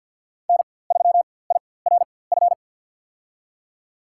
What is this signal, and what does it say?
Morse code, heard as a keyed tone, about 52 wpm, spaced as for 23 wpm, 700 Hz. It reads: N3IRF
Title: USA Call Signs - 5 Characters - 4x Character Spacing - 50wpm